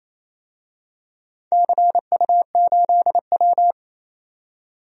Morse code, heard as a keyed tone, about 28 wpm, 700 Hz. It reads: CU8W
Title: DXCC-100 Call Signs - 4 Characters - 28wpm